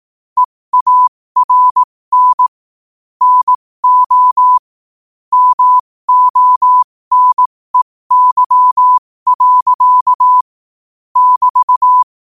Morse code, heard as a keyed tone, about 18 wpm, spaced as for 15 wpm, 1000 Hz. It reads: EARN NO MONEY. <BT>
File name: qotdfiles / 15wpm